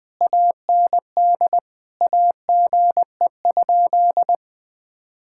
Morse code, heard as a keyed tone, about 20 wpm, 700 Hz. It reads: AND AGE?
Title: Sentences from Top 1600 Words 20wpm